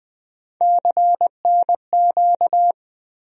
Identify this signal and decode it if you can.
Morse code, heard as a keyed tone, about 20 wpm, 700 Hz. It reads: CNQ
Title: USA Call Sign Postfixes - 3 Letters 20wpm